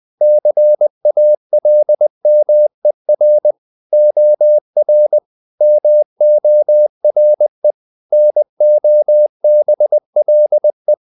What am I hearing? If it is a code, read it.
Morse code, heard as a keyed tone, about 20 wpm, 600 Hz. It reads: CALMER OR MORE NOBLE
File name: the old man and the sea part 2 20wpm